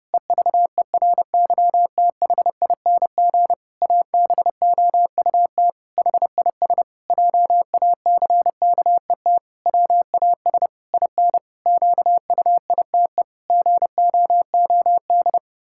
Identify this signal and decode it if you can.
Morse code, heard as a keyed tone, about 28 wpm, 700 Hz. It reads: EVERYTHING ABOUT HIS JACKET WAS IN QUITE GOOD
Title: Pangram Challenge v2 - 30wpm